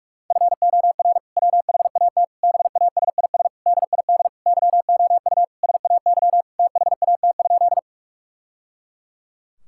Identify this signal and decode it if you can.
Morse code, heard as a keyed tone, about 42 wpm, 700 Hz. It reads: FOR WHAT BASIS DID YOU SAY THAT?
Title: Sentences from Top 1000 Words - Review Part 5 - 45wpm